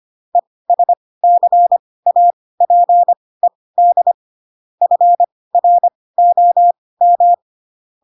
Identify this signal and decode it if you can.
Morse code, heard as a keyed tone, about 25 wpm, spaced as for 18 wpm, 700 Hz. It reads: ESCAPED FROM